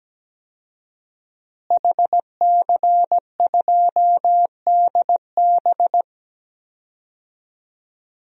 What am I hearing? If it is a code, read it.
Morse code, heard as a keyed tone, about 17 wpm, 700 Hz. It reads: HC2DB
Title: DXCC-100 Call Signs - 5 Characters - Repeated 2x - 17wpm